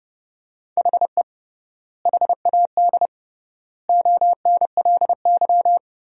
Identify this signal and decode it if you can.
Morse code, heard as a keyed tone, about 30 wpm, 700 Hz. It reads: HE HAD ONLY
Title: 3x Word-Spacing - Sentences from Top 300 Words - 30wpm